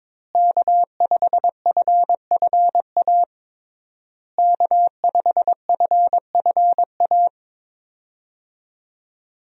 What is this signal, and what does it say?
Morse code, heard as a keyed tone, about 22 wpm, 700 Hz. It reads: K5FFA K5FFA
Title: USA Call Signs - 5 Characters - Repeated 3x - 22wpm